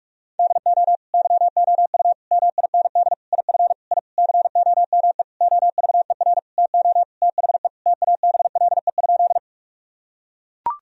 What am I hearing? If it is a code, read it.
Morse code, heard as a keyed tone, about 41 wpm, 700 Hz. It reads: DO YOU MIND IF I COME OVER TO THE TABLE?
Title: Sentences from Top 1000 Words - Review Part 2 - 45wpm